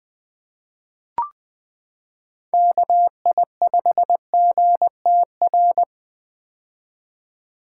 Morse code, heard as a keyed tone, about 20 wpm, 700 Hz. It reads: KI5GTR
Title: 2023 POTA Activator Call Signs - 20wpm